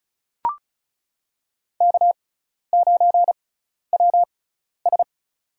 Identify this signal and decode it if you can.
Morse code, heard as a keyed tone, about 34 wpm, spaced as for 12 wpm, 700 Hz. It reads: K9WS
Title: USA Call Signs - 6x Character Spacing - 35wpm